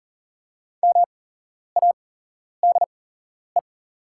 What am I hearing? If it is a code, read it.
Morse code, heard as a keyed tone, about 40 wpm, spaced as for 11 wpm, 690 Hz. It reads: MADE
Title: 8x Character Spacing - Top 200 Words - 40wpm